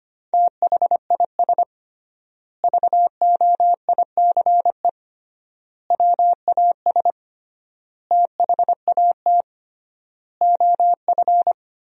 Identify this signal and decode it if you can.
Morse code, heard as a keyed tone, about 25 wpm, 700 Hz. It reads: THIS VOICE WAS THAT OF
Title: 3x Word-Spacing - Sentences from Top 500 Words - 25wpm